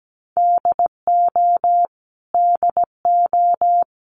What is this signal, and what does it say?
Morse code, heard as a keyed tone, about 17 wpm, 700 Hz. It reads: DO DO
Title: Top 300 Words Repeated 17wpm